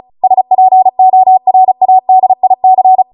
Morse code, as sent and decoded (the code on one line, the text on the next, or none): ... .--. --- .-. .- -.. .. -.-.
SPORADIC